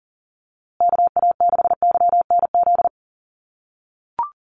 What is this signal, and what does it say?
Morse code, heard as a keyed tone, about 40 wpm, 700 Hz. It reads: KA6YNZ
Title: USA Call Signs Repeated 2x 40wpm